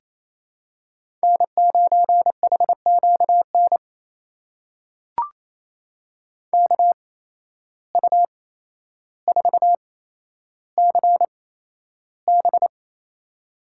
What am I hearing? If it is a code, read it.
Morse code, heard as a keyed tone, about 28 wpm, spaced as for 12 wpm, 700 Hz. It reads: N9HQN KU4CB
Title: USA Call Signs - 5 Characters - 8x Character Spacing - 28wpm